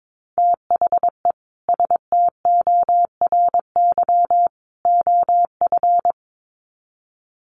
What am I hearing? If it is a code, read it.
Morse code, heard as a keyed tone, about 22 wpm, 700 Hz. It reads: THE STORY OF
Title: Sets of 3 Words 22wpm